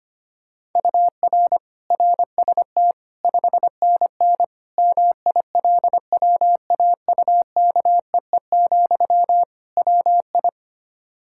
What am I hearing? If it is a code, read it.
Morse code, heard as a keyed tone, about 25 wpm, 700 Hz. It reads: UR RST 5NN MILWAUKEE, WI
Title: K1USN SST QSO Practice 25wpm